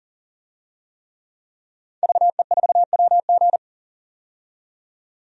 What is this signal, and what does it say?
Morse code, heard as a keyed tone, about 40 wpm, 700 Hz. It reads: VE4WG
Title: DXCC-100 Call Signs - 5 Characters - 40wpm